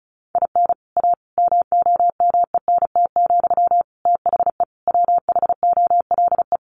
Morse code, heard as a keyed tone, about 34 wpm, 700 Hz. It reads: IN A MOMENT, THE WHOLE
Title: Sentences from Top 1000 Words 35wpm